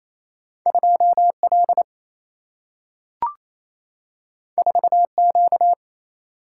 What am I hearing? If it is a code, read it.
Morse code, heard as a keyed tone, about 28 wpm, 700 Hz. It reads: 2L 4Q
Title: Single-Letter Single-Number 28wpm